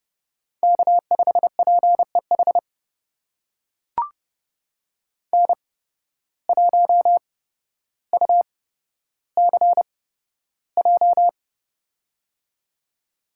Morse code, heard as a keyed tone, about 32 wpm, spaced as for 13 wpm, 700 Hz. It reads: K5PEH N1UCJ
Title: USA Call Signs - 5 Characters - 8x Character Spacing - 30wpm